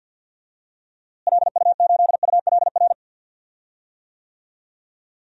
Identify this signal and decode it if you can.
Morse code, heard as a keyed tone, about 51 wpm, 700 Hz. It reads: LU8ULR
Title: DXCC-100 Call Signs - 6 Characters - Repeated 3x - 50wpm